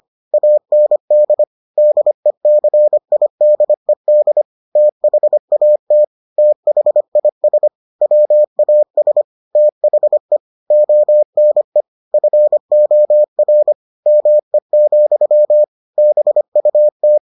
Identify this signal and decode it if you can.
Morse code, heard as a keyed tone, about 25 wpm, 600 Hz. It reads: AND DECIDED THAT THIS WAS THE ONE FOR ME, BUT